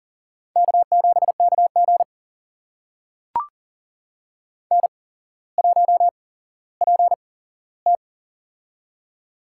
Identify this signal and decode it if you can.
Morse code, heard as a keyed tone, about 41 wpm, spaced as for 17 wpm, 700 Hz. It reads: K7KG N1PT
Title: USA Call Signs - 4 Characters - 8x Character Spacing - 40wpm